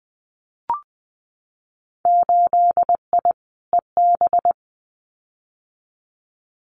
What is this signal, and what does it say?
Morse code, heard as a keyed tone, about 20 wpm, 700 Hz. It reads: 8I EB
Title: Winter Field Day Exchanges Encore 20wpm